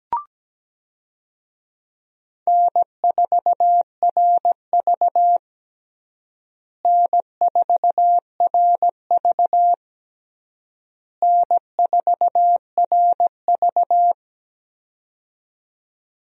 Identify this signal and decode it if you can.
Morse code, heard as a keyed tone, about 17 wpm, 700 Hz. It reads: N4RV N4RV N4RV
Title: USA Call Signs - 4 Characters - Repeated 3x - 17wpm